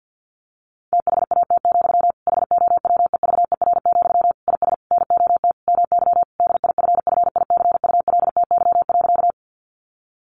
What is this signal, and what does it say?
Morse code, heard as a keyed tone, about 50 wpm, 700 Hz. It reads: THAT, HOWEVER, IS NOT MY DIFFICULTY.
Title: Sentences from Top 1500 Words 50wpm